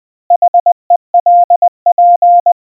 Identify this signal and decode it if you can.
Morse code, heard as a keyed tone, about 20 wpm, 700 Hz. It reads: HELP